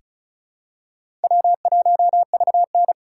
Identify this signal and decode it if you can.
Morse code, heard as a keyed tone, about 36 wpm, 700 Hz. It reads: W1VN
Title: USA Call Signs - 4 Characters - 3x Character Spacing - 35wpm